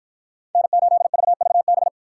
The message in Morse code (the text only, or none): N8VVB